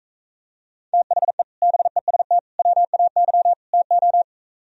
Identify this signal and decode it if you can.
Morse code, heard as a keyed tone, about 45 wpm, 700 Hz. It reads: THE BEST WAY TO